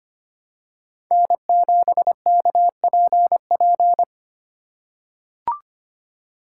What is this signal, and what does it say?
Morse code, heard as a keyed tone, about 25 wpm, 700 Hz. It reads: N7KPP
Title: USA Call Signs - 5 Characters - 4x Character Spacing - 25wpm